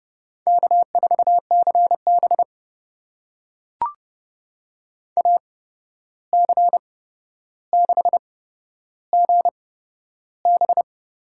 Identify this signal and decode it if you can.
Morse code, heard as a keyed tone, about 30 wpm, spaced as for 12 wpm, 700 Hz. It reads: K4CB AC6GB
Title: USA Call Signs - 8x Character Spacing - 30wpm